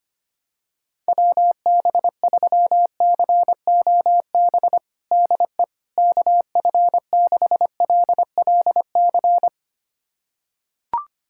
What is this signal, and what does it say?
Morse code, heard as a keyed tone, about 25 wpm, 700 Hz. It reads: WB3COB DE KF6LLC
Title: USA Callsign DE Callsign - 6 Characters - 25wpm